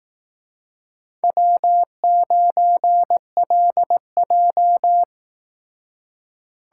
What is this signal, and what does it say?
Morse code, heard as a keyed tone, about 18 wpm, 700 Hz. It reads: W9LJ
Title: Speed Racing - USA 1x2 Call Signs 15wpm